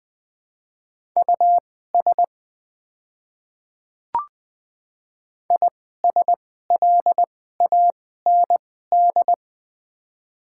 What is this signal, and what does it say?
Morse code, heard as a keyed tone, about 20 wpm, spaced as for 15 wpm, 690 Hz. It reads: US ISLAND
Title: 2x Character Spacing - Top 500 Words - 20wpm